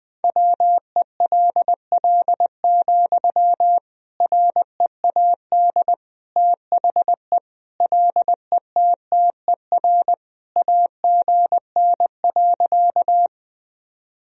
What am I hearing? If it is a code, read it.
Morse code, heard as a keyed tone, about 20 wpm, 700 Hz. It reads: WELL, READ THE LETTER AGN.
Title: Sentences from Top 300 Words and Abbreviations 20wpm